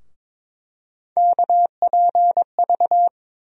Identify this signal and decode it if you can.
Morse code, heard as a keyed tone, about 22 wpm, 700 Hz. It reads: KPV
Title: USA Call Sign Postfixes - 3 Letters 22wpm